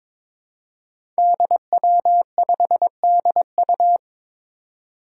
Morse code, heard as a keyed tone, about 22 wpm, 700 Hz. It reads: DW5DU
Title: Five Challenging Letter-Number Combinations 22wpm